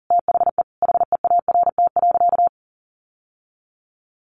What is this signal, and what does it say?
Morse code, heard as a keyed tone, about 40 wpm, 700 Hz. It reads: THE HEART.
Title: Sentences from Top 1200 Words 40wpm